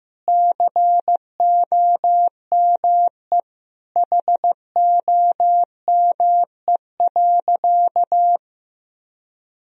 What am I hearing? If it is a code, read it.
Morse code, heard as a keyed tone, about 16 wpm, 700 Hz. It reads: COME HOME.
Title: Sentences from Top 1000 Words - Review Part 3 - 15wpm